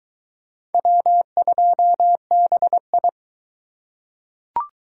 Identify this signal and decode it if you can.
Morse code, heard as a keyed tone, about 22 wpm, 700 Hz. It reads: W2BI